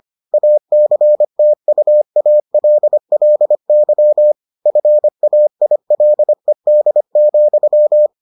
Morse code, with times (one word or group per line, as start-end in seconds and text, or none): 0.3-4.3 ACTUALLY
4.7-8.1 FAILED,